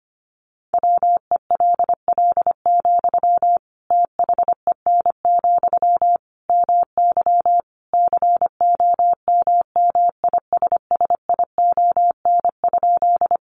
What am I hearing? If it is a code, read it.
Morse code, heard as a keyed tone, about 25 wpm, 700 Hz. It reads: WELL, THEN, MY COMMISSION?
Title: Sentences from Top 1400 Words 25wpm